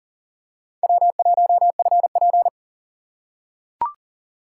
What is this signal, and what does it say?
Morse code, heard as a keyed tone, about 39 wpm, 700 Hz. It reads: W1FP
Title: USA Call Signs - 4 Characters - 6x Character Spacing - 40wpm